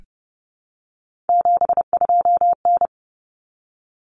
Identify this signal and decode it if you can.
Morse code, heard as a keyed tone, about 30 wpm, 700 Hz. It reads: 72N
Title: Common Cut Numbers - 3-Digit 30wpm